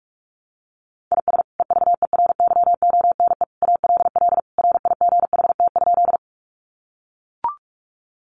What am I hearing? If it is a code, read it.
Morse code, heard as a keyed tone, about 44 wpm, 700 Hz. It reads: IS EVERYONE ALL RIGHT?
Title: Sentences from Top 800 Words 45wpm